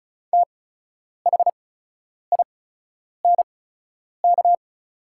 Morse code, heard as a keyed tone, about 36 wpm, spaced as for 10 wpm, 690 Hz. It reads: THINK
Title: 8x Character Spacing - Top 200 Words - 35wpm